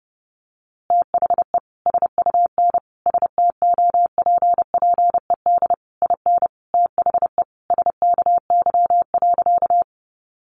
Morse code, heard as a keyed tone, about 30 wpm, 700 Hz. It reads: THE SUN STOPPED IN THE SKY.
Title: A Sound Of Thunder 30wpm